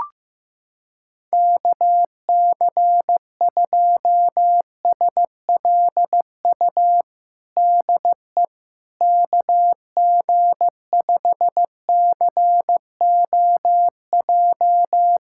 Morse code, heard as a keyed tone, about 15 wpm, 700 Hz. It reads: KC2SLU DE KG5COJ
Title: USA Callsign DE Callsign - 6 Characters - 15wpm